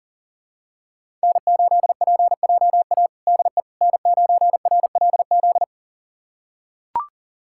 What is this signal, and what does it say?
Morse code, heard as a keyed tone, about 40 wpm, 700 Hz. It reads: N8PJA DE N9RLZ
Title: USA Callsign DE Callsign - 5 Characters - 40wpm